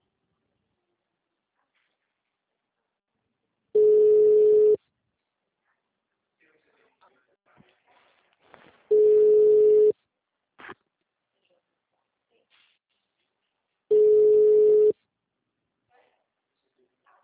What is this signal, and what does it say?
Ringback tone (Europe, 425 Hz), with speech.